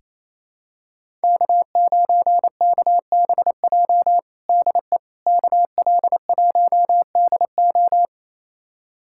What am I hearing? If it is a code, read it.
Morse code, heard as a keyed tone, about 28 wpm, 700 Hz. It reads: K9KBJ DE KL1DO
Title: USA Callsign DE Callsign - 5 Characters - 28wpm